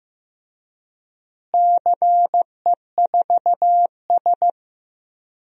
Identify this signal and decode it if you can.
Morse code, heard as a keyed tone, about 15 wpm, 700 Hz. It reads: CE4S
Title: DXCC-100 Call Signs - 4 Characters - Repeated 2x - 15wpm